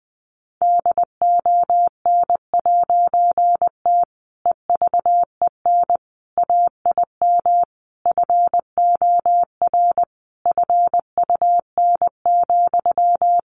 Morse code, heard as a keyed tone, about 20 wpm, 700 Hz. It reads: DON'T EVEN AIM FOR FUN,
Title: A Sound Of Thunder 20wpm